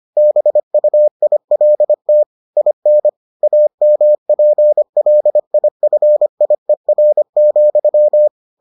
Morse code, heard as a keyed tone, about 25 wpm, 600 Hz. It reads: BUILT IN AMPLIFIER,